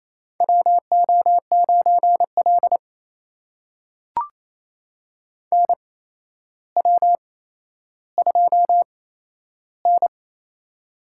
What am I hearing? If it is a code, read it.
Morse code, heard as a keyed tone, about 28 wpm, spaced as for 12 wpm, 700 Hz. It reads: WO9L NW2N
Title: USA Call Signs - 4 Characters - 8x Character Spacing - 28wpm